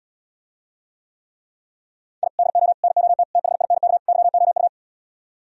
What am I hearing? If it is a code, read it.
Morse code, heard as a keyed tone, about 77 wpm, 700 Hz. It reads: I DO NOT THINK YOU